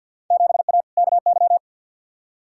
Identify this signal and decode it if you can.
Morse code, heard as a keyed tone, about 52 wpm, 700 Hz. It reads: 7A KY
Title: Field Day Exchanges Encore - 50wpm